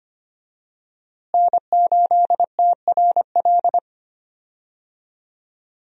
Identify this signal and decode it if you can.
Morse code, heard as a keyed tone, about 25 wpm, 700 Hz. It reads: N8TRL
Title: USA Call Signs - 5 Characters - Repeated 2x - 25wpm